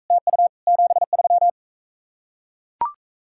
Morse code, heard as a keyed tone, about 43 wpm, 700 Hz. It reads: TU 73